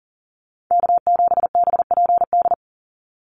Morse code, heard as a keyed tone, about 42 wpm, 700 Hz. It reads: K7BPD
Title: USA Call Signs 40wpm